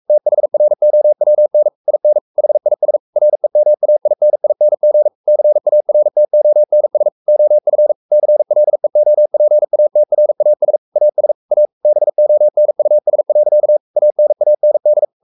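Morse code, heard as a keyed tone, about 44 wpm, 600 Hz. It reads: THROWN IN HIS REMAINING CARTONS OF CLEOPATRAS AS A BONUS. ANAND
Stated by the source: Morse code at 43 wpm sending Ken Follett's novel Never in English, from CW med Gnister.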